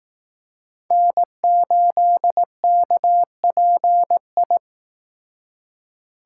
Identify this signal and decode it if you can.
Morse code, heard as a keyed tone, about 18 wpm, 700 Hz. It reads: N8KPI
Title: Speed Racing - USA 1x3 Call Signs 15wpm